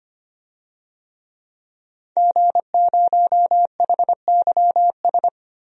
Morse code, heard as a keyed tone, about 25 wpm, 700 Hz. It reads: G0HYS
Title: DXCC-100 Call Signs - 5 Characters - Repeated 3x - 25wpm